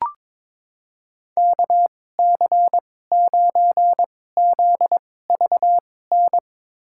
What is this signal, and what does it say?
Morse code, heard as a keyed tone, about 22 wpm, spaced as for 16 wpm, 700 Hz. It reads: KC9ZVN
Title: USA Call Signs - 6 Characters - 2x Character Spacing - 22wpm